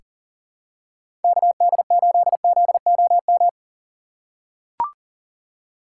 Morse code, heard as a keyed tone, about 40 wpm, 700 Hz. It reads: KD8ZOM
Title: USA Call Signs - 6 Characters - 4x Character Spacing - 40wpm